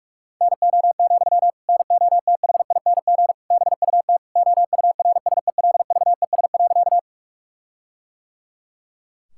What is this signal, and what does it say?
Morse code, heard as a keyed tone, about 43 wpm, 700 Hz. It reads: NO, NOTHING BUT OURSELVES.
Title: Sentences from Top 1600 Words 45wpm